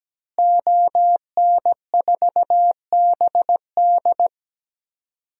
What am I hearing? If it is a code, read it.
Morse code, heard as a keyed tone, about 17 wpm, 700 Hz. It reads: ON4BD
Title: DXCC-100 Call Signs - 5 Characters - Repeated 2x - 17wpm